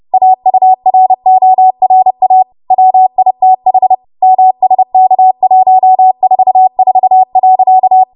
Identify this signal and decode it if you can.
Morse code, heard as a keyed tone, about 30 wpm, 750 Hz. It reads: AURORA WITH MSK144.
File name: Morse / 30WPM